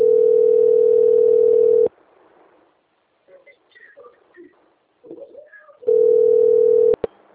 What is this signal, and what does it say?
Ringback tone (North America), 440 + 480 Hz.
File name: out- - - - - .3407